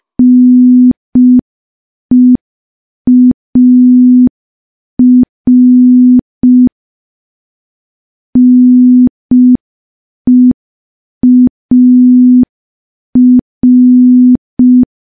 Morse code, heard as a keyed tone, about 5 wpm, 250 Hz. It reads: NEAR NEAR
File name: morse / morse-training-05WPM